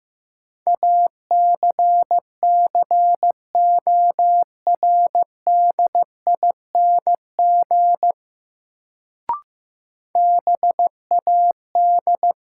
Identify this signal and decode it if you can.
Morse code, heard as a keyed tone, about 15 wpm, 700 Hz. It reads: ACCORDING BAD
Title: Sentences from Top 700 Words - New Word Drill - 15wpm